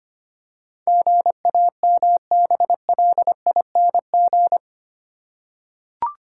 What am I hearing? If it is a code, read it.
Morse code, heard as a keyed tone, about 25 wpm, 700 Hz. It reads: GAMBLING
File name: 8 Letter Words - First 3 Letters Given - 25wpm